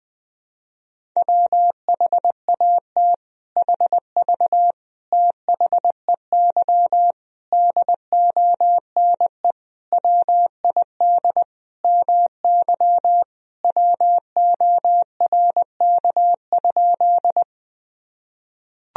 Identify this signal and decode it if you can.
Morse code, heard as a keyed tone, about 21 wpm, 700 Hz. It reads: WHAT HV THEY DONE WID MY WORK?
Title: Sentences from Top 500 Words and Abbreviations 20wpm